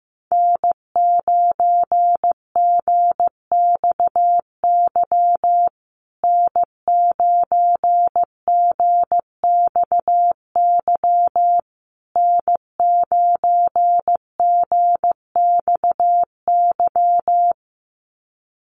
Morse code, heard as a keyed tone, about 15 wpm, 700 Hz. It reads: N9GXY N9GXY N9GXY
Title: USA Call Signs Repeated 3x 15wpm